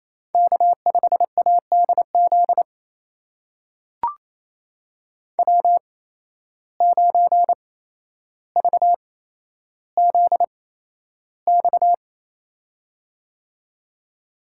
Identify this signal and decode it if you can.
Morse code, heard as a keyed tone, about 28 wpm, spaced as for 12 wpm, 700 Hz. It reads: K5ADZ W9VZX
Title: USA Call Signs - 5 Characters - 8x Character Spacing - 28wpm